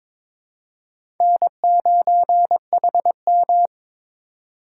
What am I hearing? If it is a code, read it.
Morse code, heard as a keyed tone, about 22 wpm, 700 Hz. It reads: N9HM
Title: USA Call Signs - 4 Characters - 2x Character Spacing - 22wpm